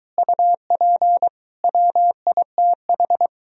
Morse code, heard as a keyed tone, about 23 wpm, 700 Hz. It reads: UP WITH